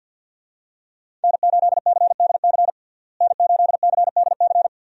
Morse code, heard as a keyed tone, about 48 wpm, 700 Hz. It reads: N8CDC N8CDC
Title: USA Call Signs - 5 Characters - Repeated 2x - 50wpm